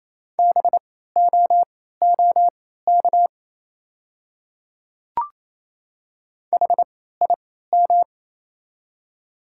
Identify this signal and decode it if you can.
Morse code, heard as a keyed tone, about 28 wpm, spaced as for 16 wpm, 700 Hz. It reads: BOOK HIM
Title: 3x Character Spacing - Top 300 Words - 28wpm